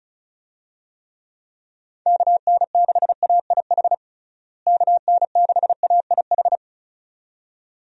Morse code, heard as a keyed tone, about 34 wpm, 700 Hz. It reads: KN6AIH KN6AIH
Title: USA Call Signs - 6 Characters - Repeated 2x - 35wpm